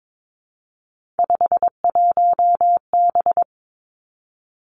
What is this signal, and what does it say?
Morse code, heard as a keyed tone, about 22 wpm, 700 Hz. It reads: 51B
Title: Three Challenging Letter-Number Combinations 22wpm